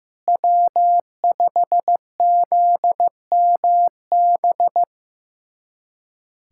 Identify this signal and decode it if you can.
Morse code, heard as a keyed tone, about 15 wpm, 700 Hz. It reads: W5ZMB
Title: USA Callsign DE Callsign - 5 Characters - 15wpm